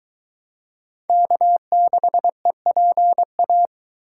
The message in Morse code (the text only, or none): K6EPA